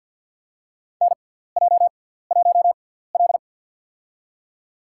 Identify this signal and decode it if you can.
Morse code, heard as a keyed tone, about 52 wpm, spaced as for 17 wpm, 700 Hz. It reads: NJ1L
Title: USA Call Signs - 4 Characters - 6x Character Spacing - 50wpm